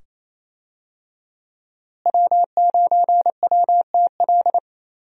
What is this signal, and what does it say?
Morse code, heard as a keyed tone, about 28 wpm, 700 Hz. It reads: W9WTL
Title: USA Call Signs - 5 Characters - Repeated 2x - 28wpm